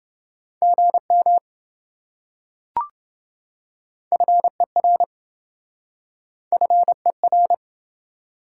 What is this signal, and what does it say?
Morse code, heard as a keyed tone, about 27 wpm, 700 Hz. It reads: GM FER FER